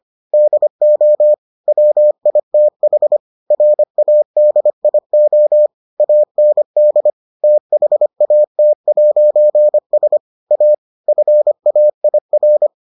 Morse code, heard as a keyed tone, about 25 wpm, 600 Hz. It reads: DO WITH RADIO AND THAT'S A FAIR